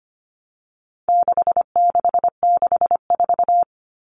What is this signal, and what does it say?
Morse code, heard as a keyed tone, about 25 wpm, 700 Hz. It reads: 6664
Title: Numbers - 4-Digit 25wpm